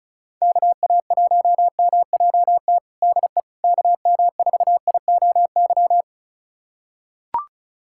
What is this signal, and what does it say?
Morse code, heard as a keyed tone, about 36 wpm, 700 Hz. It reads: KA1MJT DE KM4IOY
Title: USA Callsign DE Callsign - 6 Characters - 35wpm